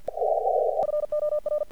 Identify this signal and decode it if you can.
Morse code, heard as a keyed tone, about 50 wpm, 1200 Hz. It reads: FOR
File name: archived / CRSnet